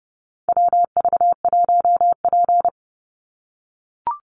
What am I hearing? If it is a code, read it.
Morse code, heard as a keyed tone, about 30 wpm, 700 Hz. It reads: WV1P
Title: USA Call Signs 30wpm